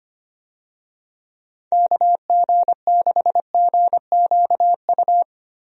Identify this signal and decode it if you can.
Morse code, heard as a keyed tone, about 25 wpm, 700 Hz. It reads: KG6GQU
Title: USA Call Signs - 6 Characters - Repeated 3x - 25wpm